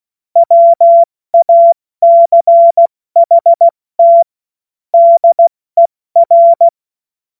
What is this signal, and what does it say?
Morse code, heard as a keyed tone, about 16 wpm, 680 Hz. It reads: WACHT DER
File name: Morsekurs / Witze / Witz 14. Juni 2022 70 F80